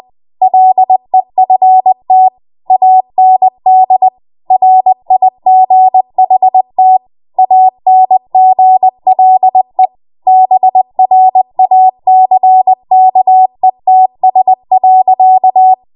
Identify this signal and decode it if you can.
Morse code, heard as a keyed tone, about 20 wpm, 750 Hz. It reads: LEFT AND RIGHT ANGLE BRACKETS.
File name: Morse / 20WPM